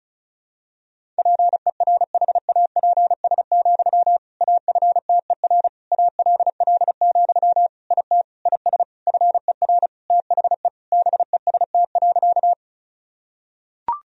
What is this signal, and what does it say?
Morse code, heard as a keyed tone, about 34 wpm, 700 Hz. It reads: PERHAPS, AFTER ALL, IT IS FER THE BEST.
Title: Sentences from Top 500 Words and Abbreviations 35wpm